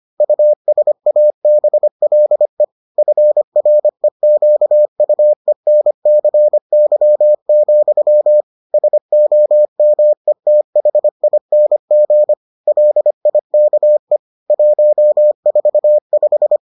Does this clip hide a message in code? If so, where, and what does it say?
Yes: Morse code from 0.2 to 16.6 s, heard as a keyed tone, about 25 wpm, 600 Hz. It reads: USABLE FREQUENCY, SOMETHING LIKE 145